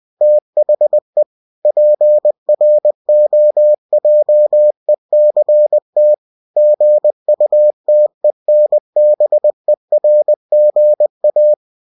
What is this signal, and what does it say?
Morse code, heard as a keyed tone, about 20 wpm, 600 Hz. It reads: THE PROJECT GUTENBERGA
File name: cw-books / winnie-the-pooh / winnie-20wpm-page-0120